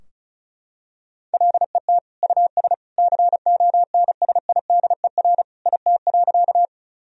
Morse code, heard as a keyed tone, about 34 wpm, 700 Hz. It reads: LET US CONSIDER IT.
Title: Sentences from Top 1000 Words - Review Part 4 - 35wpm